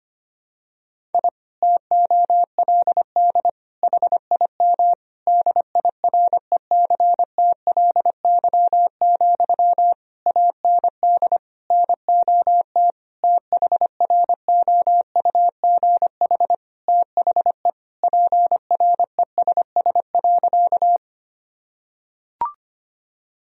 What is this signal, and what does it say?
Morse code, heard as a keyed tone, about 25 wpm, 700 Hz. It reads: I TOLD HIM DIRECTLY, AND NOT THROUGH THE PRESS.
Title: Sentences from Top 1000 Words - Review Part 1 - 25wpm